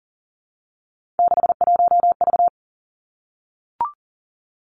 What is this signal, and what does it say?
Morse code, heard as a keyed tone, about 40 wpm, 700 Hz. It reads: 61V
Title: Three Challenging Letter-Number Combinations 40wpm